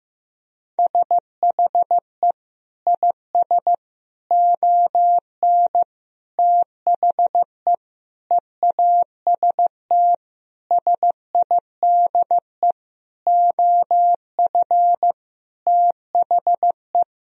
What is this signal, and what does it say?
Morse code, heard as a keyed tone, about 15 wpm, 700 Hz. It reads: SHE IS ON THE EAST SIDE OF THE
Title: Sentences from Top 1000 Words - Review Part 3 - 15wpm